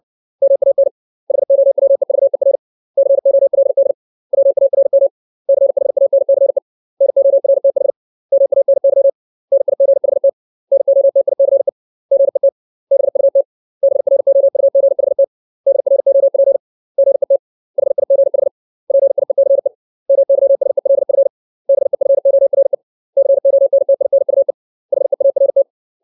Morse code, heard as a keyed tone, about 60 wpm, 550 Hz. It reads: MAN HOWEVER COLD WARM CHANCE NORTH MANY NIGHT NOTICE GET BUT BROUGHT DROP GET HIGH PIECE MYSELF BEFORE CONTINUE HEART
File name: 632 words / 632 Words 60WPM